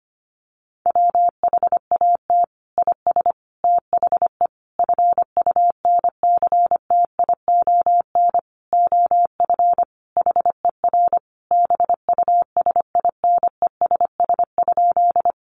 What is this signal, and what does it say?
Morse code, heard as a keyed tone, about 25 wpm, 700 Hz. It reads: WHAT IS THE FUNCTION OF HER BUSINESS?
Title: Sentences from Top 1100 Words 25wpm